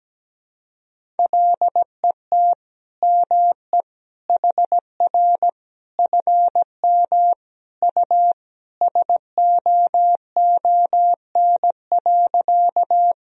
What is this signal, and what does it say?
Morse code, heard as a keyed tone, about 17 wpm, 700 Hz. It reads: LET ME HR FM U SOON.
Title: Sentences from Top 400 Words and Abbreviations 17wpm